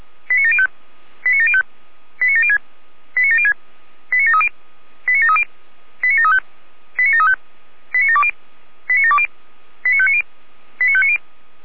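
Five-tone selective call (ZVEI), heard as quick runs of five tones, tones of about 70 ms. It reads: 89705 89705 89706 89706 89730 89730 89735 89735 89720 89720 89590 89590
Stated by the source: Five-tone ZVEI selective call received on an SDR.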